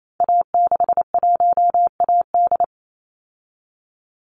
Morse code, heard as a keyed tone, about 28 wpm, 700 Hz. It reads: A61AD
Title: Call Signs Repeated 2x - 28wpm